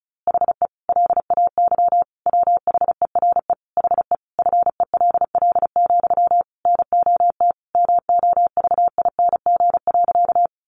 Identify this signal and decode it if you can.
Morse code, heard as a keyed tone, about 34 wpm, 700 Hz. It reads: HE LAY WHERE HE FELL, NOT MOVING.
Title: A Sound Of Thunder 35wpm